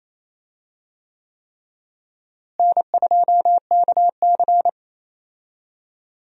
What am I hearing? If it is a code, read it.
Morse code, heard as a keyed tone, about 28 wpm, 700 Hz. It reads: N2KC
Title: USA Call Signs - 4 Characters - Repeated 3x - 28wpm